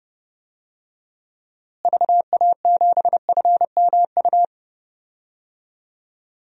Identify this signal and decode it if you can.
Morse code, heard as a keyed tone, about 32 wpm, 700 Hz. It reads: VA7FMU
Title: DXCC-100 Call Signs - 6 Characters - Repeated 2x - 30wpm